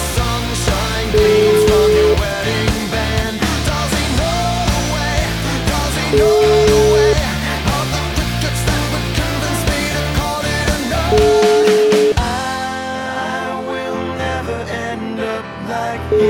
Ringback tone (Europe, 425 Hz), with noise.